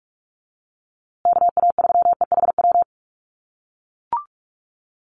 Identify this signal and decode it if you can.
Morse code, heard as a keyed tone, about 44 wpm, 700 Hz. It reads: KA3EHW
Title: USA Call Signs Repeated 3x 45wpm